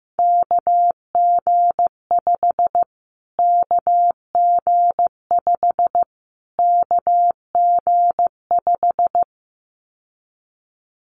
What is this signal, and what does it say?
Morse code, heard as a keyed tone, about 15 wpm, 700 Hz. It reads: KG5 KG5 KG5
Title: Callsign Prefixes for Top 25 DXCC - Repeated 3x - 15wpm